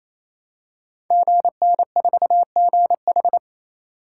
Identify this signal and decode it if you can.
Morse code, heard as a keyed tone, about 28 wpm, 700 Hz. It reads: GN4GH
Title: Code Groups - 5 Letter:Numbers - 28wpm